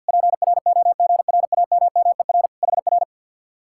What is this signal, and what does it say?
Morse code, heard as a keyed tone, about 59 wpm, 700 Hz. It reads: PROGRAMMER HR
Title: Speed-Racing Common QSO Phrases Encore - 50wpm